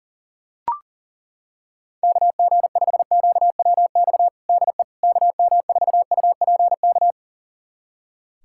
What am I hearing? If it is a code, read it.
Morse code, heard as a keyed tone, about 40 wpm, 700 Hz. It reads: KG5QWX DE KM4UPK
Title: USA Callsign DE Callsign - 6 Characters - 40wpm